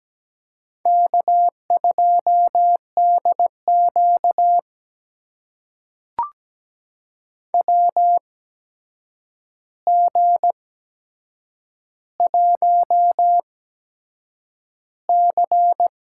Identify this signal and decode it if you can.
Morse code, heard as a keyed tone, about 17 wpm, spaced as for 7 wpm, 700 Hz. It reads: K2DQ WG1C
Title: USA Call Signs - 8x Character Spacing - 17wpm